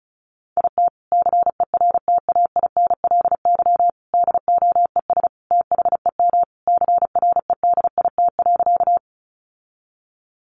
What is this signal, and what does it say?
Morse code, heard as a keyed tone, about 34 wpm, 700 Hz. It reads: IT CERTAINLY DOES THEM CREDIT.
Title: Sentences from Top 1000 Words 35wpm